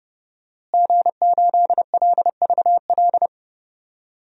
Morse code, heard as a keyed tone, about 30 wpm, 700 Hz. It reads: G8LVL
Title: DXCC-100 Call Signs - 5 Characters - Repeated 3x - 30wpm